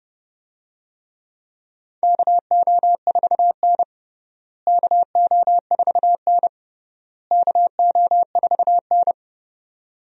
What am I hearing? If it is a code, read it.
Morse code, heard as a keyed tone, about 30 wpm, 700 Hz. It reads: KO4N KO4N KO4N
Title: USA Call Signs - 4 Characters - Repeated 3x - 30wpm